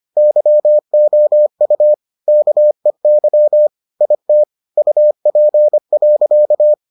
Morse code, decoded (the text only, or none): YOU KEY IT UP.